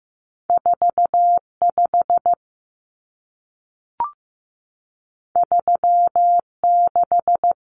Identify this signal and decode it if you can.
Morse code, heard as a keyed tone, about 15 wpm, 700 Hz. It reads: 45 36
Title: Common Cut Numbers - 2-Digit 15wpm